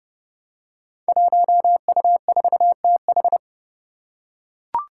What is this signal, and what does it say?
Morse code, heard as a keyed tone, about 28 wpm, 700 Hz. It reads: 1U4TH
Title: Five Challenging Letter-Number Combinations 30wpm